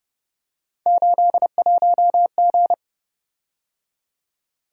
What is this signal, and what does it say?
Morse code, heard as a keyed tone, about 29 wpm, 700 Hz. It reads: 81G
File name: Speed-Racing - Three Letter-Number Combinations - 25wpm